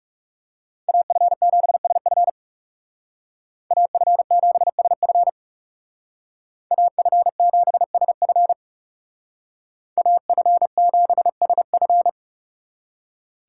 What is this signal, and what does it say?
Morse code, heard as a keyed tone, about 36 wpm, 700 Hz. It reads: AF7SF AF7SF AF7SF AF7SF
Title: Speed Racing - USA 2x2 Call Signs 30wpm